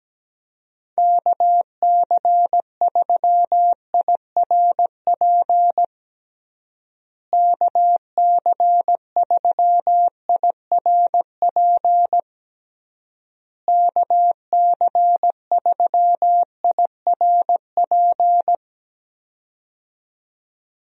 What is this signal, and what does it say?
Morse code, heard as a keyed tone, about 17 wpm, 700 Hz. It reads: KC3IRP KC3IRP KC3IRP
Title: USA Call Signs - 6 Characters - Repeated 3x - 17wpm